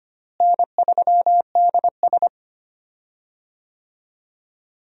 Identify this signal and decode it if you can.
Morse code, heard as a keyed tone, about 25 wpm, 700 Hz. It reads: N3DS
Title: USA Call Signs - 4 Characters - Repeated 2x - 25wpm